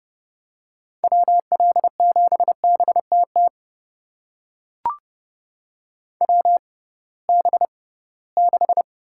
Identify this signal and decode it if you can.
Morse code, heard as a keyed tone, about 32 wpm, spaced as for 20 wpm, 700 Hz. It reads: WL7BTT WB6
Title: USA Call Signs - 6 Characters - 6x Character Spacing - 30wpm